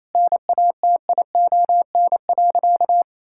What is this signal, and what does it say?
Morse code, heard as a keyed tone, about 28 wpm, 700 Hz. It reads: NATION.
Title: Sentences from Top 1000 Words - Review Part 6 - 28wpm